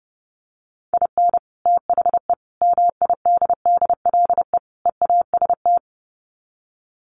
Morse code, heard as a keyed tone, about 30 wpm, 700 Hz. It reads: IN THE MIDDLE EAST
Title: Sets of 4 Words 30wpm